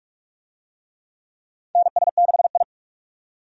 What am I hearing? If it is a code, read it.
Morse code, heard as a keyed tone, about 46 wpm, 700 Hz. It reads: NS6I
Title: USA Call Signs - 4 Characters - 45wpm